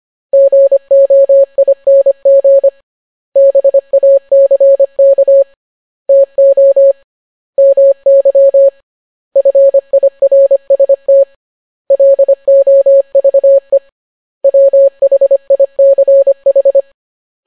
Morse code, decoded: GOING BACK TO MY FIRST LOVE WHICH